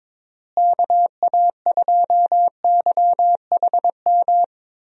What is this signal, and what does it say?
Morse code, heard as a keyed tone, about 22 wpm, 700 Hz. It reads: KA2YHM
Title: USA Call Signs - 6 Characters - Repeated 3x - 22wpm